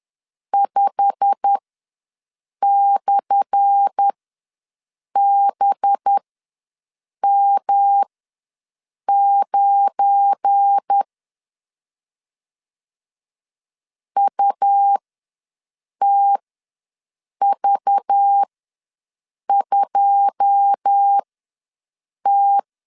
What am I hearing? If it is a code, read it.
Morse code, heard as a keyed tone, about 10 wpm, spaced as for 6 wpm, 800 Hz. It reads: 5/BM9 UTV2T
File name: LEZIONI / 10 SIGNS / 10A SLASH ws 10wpm 4121